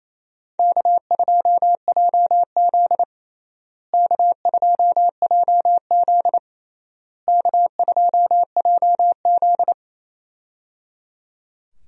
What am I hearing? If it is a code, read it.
Morse code, heard as a keyed tone, about 27 wpm, 700 Hz. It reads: K2JZ K2JZ K2JZ
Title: USA Call Signs - 4 Characters - Repeated 3x - 28wpm